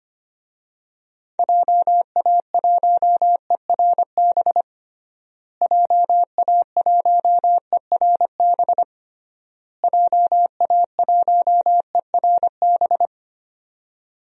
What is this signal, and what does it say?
Morse code, heard as a keyed tone, about 25 wpm, 700 Hz. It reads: JA1ERB JA1ERB JA1ERB
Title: DXCC-100 Call Signs - 6 Characters - Repeated 3x - 25wpm